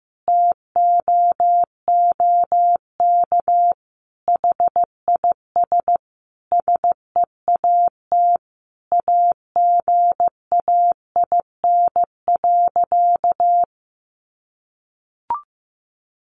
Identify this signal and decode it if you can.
Morse code, heard as a keyed tone, about 15 wpm, 700 Hz. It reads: TOOK HIS SEAT AGAIN.
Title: Sentences from Top 1500 Words 15wpm